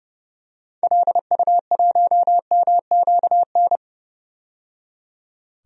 Morse code, heard as a keyed tone, about 30 wpm, 700 Hz. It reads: LU1MQN
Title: DXCC-100 Call Signs - 6 Characters - 30wpm